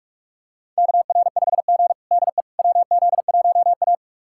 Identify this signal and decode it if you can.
Morse code, heard as a keyed tone, about 44 wpm, 690 Hz. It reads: KR5G DE WZ1A
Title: USA Callsign DE Callsign - 4 Characters - 45wpm